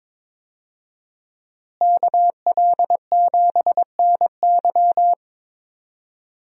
Morse code, heard as a keyed tone, about 22 wpm, 700 Hz. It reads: KL7NY
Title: USA Call Signs - 5 Characters - Repeated 3x - 22wpm